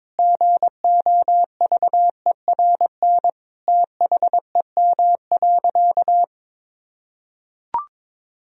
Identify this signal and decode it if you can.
Morse code, heard as a keyed tone, about 22 wpm, 700 Hz. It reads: GOVERN THEM.
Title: Sentences from Top 1000 Words - Review Part 2 - 22wpm